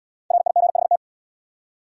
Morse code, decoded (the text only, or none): RIGHT